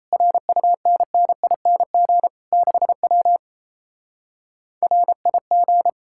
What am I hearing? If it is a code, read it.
Morse code, heard as a keyed tone, about 37 wpm, 700 Hz. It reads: RUNNING 6W RIG